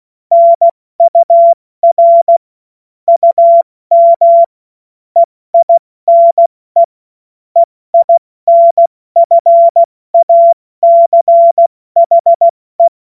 Morse code, heard as a keyed tone, about 16 wpm, 680 Hz. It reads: NUR UM EINE EINFACHE
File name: Morsekurs / Witze / Witz 20. Juni 2023 70 F80